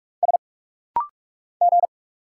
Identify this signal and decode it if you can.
Morse code, heard as a keyed tone, about 46 wpm, 700 Hz. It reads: S G
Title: Single Letters - Rapid-Fire 45wpm